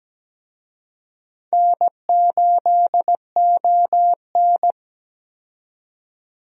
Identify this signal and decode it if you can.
Morse code, heard as a keyed tone, about 17 wpm, 700 Hz. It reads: N8ON
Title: USA Call Signs - 4 Characters - 17wpm